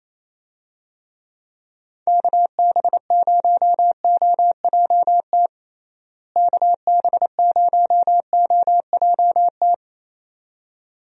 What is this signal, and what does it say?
Morse code, heard as a keyed tone, about 28 wpm, 700 Hz. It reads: KB0OJT KB0OJT
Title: USA Call Signs - 6 Characters - Repeated 2x - 28wpm